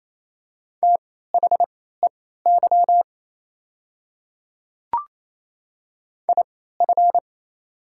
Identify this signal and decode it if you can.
Morse code, heard as a keyed tone, about 28 wpm, spaced as for 16 wpm, 690 Hz. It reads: THEY IF
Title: 3x Character Spacing - Top 100 Words - 28wpm